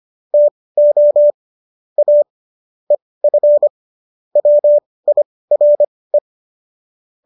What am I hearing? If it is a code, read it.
Morse code, heard as a keyed tone, about 25 wpm, spaced as for 18 wpm, 590 Hz. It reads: TO A EF WIRE
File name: qso203 18